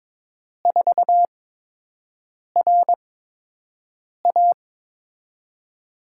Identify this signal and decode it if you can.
Morse code, heard as a keyed tone, about 23 wpm, spaced as for 6 wpm, 700 Hz. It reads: 4RA
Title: USA Call Signs - 5 Characters - 8x Character Spacing - 22wpm